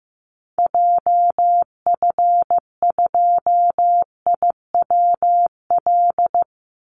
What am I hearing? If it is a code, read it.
Morse code, heard as a keyed tone, about 15 wpm, 700 Hz. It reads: JF2IWL
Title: Call Signs Repeated 2x - 15wpm